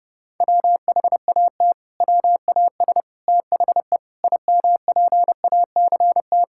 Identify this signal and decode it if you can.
Morse code, heard as a keyed tone, about 30 wpm, 700 Hz. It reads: WHAT WAS THE IMPACT
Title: Sentences from Top 1000 Words - Review Part 6 - 30wpm